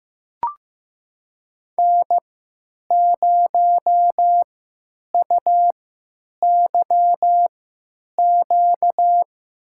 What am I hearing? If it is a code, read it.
Morse code, heard as a keyed tone, about 15 wpm, spaced as for 9 wpm, 700 Hz. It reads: N0UYQ
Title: USA Call Signs - 5 Characters - 3x Character Spacing - 15wpm